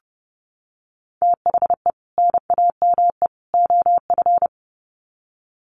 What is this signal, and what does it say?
Morse code, heard as a keyed tone, about 30 wpm, 700 Hz. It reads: THE NAME OF
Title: Sets of 3 Words 30wpm